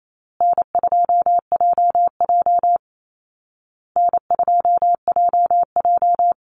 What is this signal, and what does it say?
Morse code, heard as a keyed tone, about 28 wpm, 700 Hz. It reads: N2JJ N2JJ
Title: Easy Call Signs Repeated 3x - 28wpm